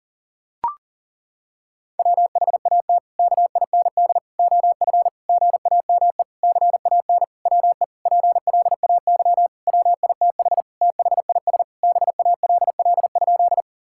Morse code, heard as a keyed tone, about 40 wpm, 700 Hz. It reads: WHAT KIND OF GAME CAN WE PLAY WITH THIS BALL?
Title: Sentences from Top 1000 Words - Review Part 6 - 40wpm